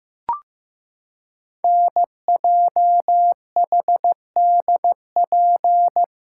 Morse code, heard as a keyed tone, about 15 wpm, 700 Hz. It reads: NJHDP
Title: Code Groups - 5 Letters - 15wpm